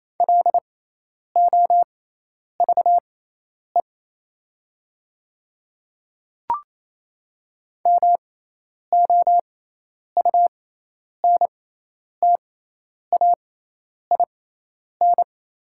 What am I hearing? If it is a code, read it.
Morse code, heard as a keyed tone, about 27 wpm, spaced as for 10 wpm, 700 Hz. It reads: LOVE MOUNTAIN
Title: 6x Character Spacing - Top 500 Words - 28wpm